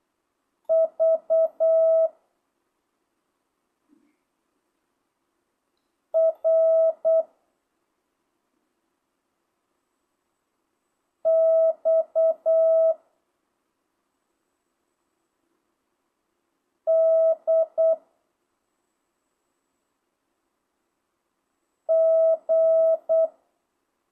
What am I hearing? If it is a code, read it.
Morse code, heard as a keyed tone, about 8 wpm, 650 Hz. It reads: V R X D G
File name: jh8cbh / 2025cw